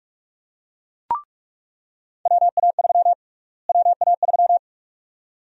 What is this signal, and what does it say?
Morse code, heard as a keyed tone, about 41 wpm, 700 Hz. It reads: WA3 WA3
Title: USA Call Sign Prefixes - Repeated 2x - 45wpm